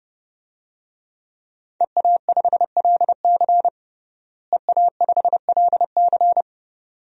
Morse code, heard as a keyed tone, about 30 wpm, 700 Hz. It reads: EA5LC EA5LC
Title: DXCC-100 Call Signs - 5 Characters - Repeated 2x - 30wpm